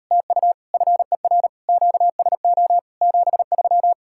Morse code, heard as a keyed tone, about 38 wpm, 700 Hz. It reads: TU FER QSO 73